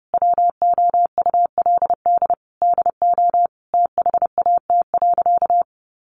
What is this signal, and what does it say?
Morse code, heard as a keyed tone, about 30 wpm, 700 Hz. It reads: WOULD DO THAT.
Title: Sentences from Top 800 Words 30wpm